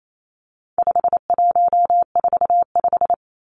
Morse code, heard as a keyed tone, about 28 wpm, 700 Hz. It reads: 5145
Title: Numbers - 4-Digit 28wpm